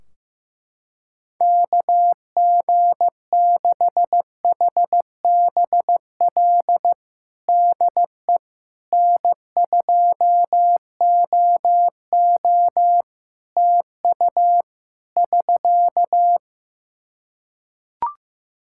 Morse code, heard as a keyed tone, about 15 wpm, 700 Hz. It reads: KG6HBL DE N2OO TU <SK>
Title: K1USN SST QSO Practice 15wpm